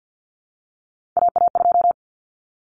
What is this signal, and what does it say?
Morse code, heard as a keyed tone, about 51 wpm, 700 Hz. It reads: AA2
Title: Common Cut Numbers - 3-Digit 50wpm